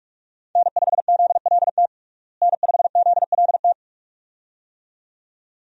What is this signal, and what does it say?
Morse code, heard as a keyed tone, about 42 wpm, 700 Hz. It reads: N5ZLT N5ZLT
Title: USA Call Signs - 5 Characters - Repeated 2x - 45wpm